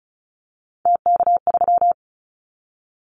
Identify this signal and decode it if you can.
Morse code, heard as a keyed tone, about 36 wpm, 700 Hz. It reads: TK3
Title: Callsign Prefixes for Top 100 DXCC - 35wpm